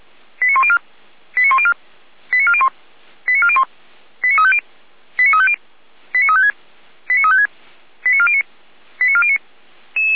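Five-tone selective call (ZVEI), heard as quick runs of five tones, tones of about 80 ms. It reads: 89104 89104 89401 89401 89350 89350 89356 89356 89409 89409
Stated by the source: ZVEI five-tone call 89338 received on SDR.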